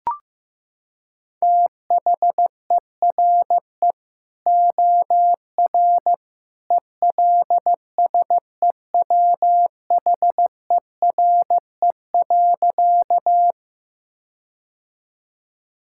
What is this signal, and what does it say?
Morse code, heard as a keyed tone, about 15 wpm, 700 Hz. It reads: THERE OR ELSEWHERE.
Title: Sentences from Top 1600 Words 15wpm